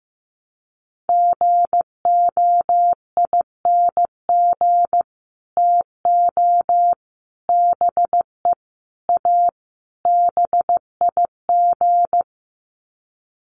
Morse code, heard as a keyed tone, about 15 wpm, 700 Hz. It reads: GOING TO BE A BIG
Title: Sets of 5 Words Encore2 15wpm